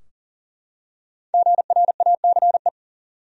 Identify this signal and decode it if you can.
Morse code, heard as a keyed tone, about 40 wpm, 700 Hz. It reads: GRACE
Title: USA Common Names - Max 5 Letters - 40wpm